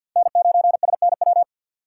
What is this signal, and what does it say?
Morse code, heard as a keyed tone, about 53 wpm, 700 Hz. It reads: N9SNW
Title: USA Call Signs - 5 Characters - 2x Character Spacing - 50wpm